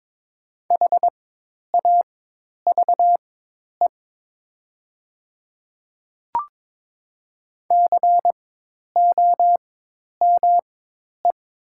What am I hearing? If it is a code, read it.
Morse code, heard as a keyed tone, about 22 wpm, spaced as for 10 wpm, 700 Hz. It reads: HAVE COME
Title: 4x Character Spacing - Top 100 Words - 22wpm